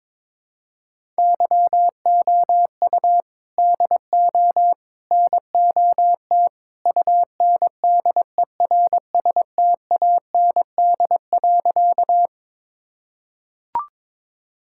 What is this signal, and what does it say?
Morse code, heard as a keyed tone, about 22 wpm, 700 Hz. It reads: YOU DO NOT UNDERSTAND.
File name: practice / 22wpm / Sentences with 3-Word Phrases - 22wpm